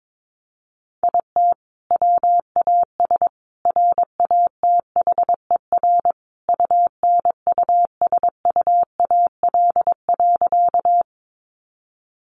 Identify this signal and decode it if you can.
Morse code, heard as a keyed tone, about 22 wpm, 700 Hz. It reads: IT WAS RATHER UNUSUAL.
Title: Sentences from Top 1400 Words 22wpm